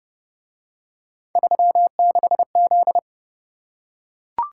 Morse code, heard as a keyed tone, about 29 wpm, 700 Hz. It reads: 36Z